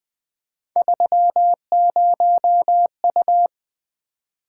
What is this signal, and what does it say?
Morse code, heard as a keyed tone, about 20 wpm, 700 Hz. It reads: 30U
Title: Three Letter-Number Combinations - 20wpm